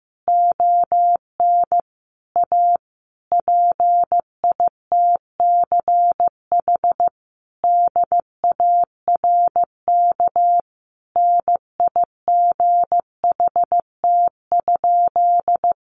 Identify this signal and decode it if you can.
Morse code, heard as a keyed tone, about 15 wpm, 700 Hz. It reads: ON A PITCH DARK NIGHT?
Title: Sentences from Top 1300 Words 15wpm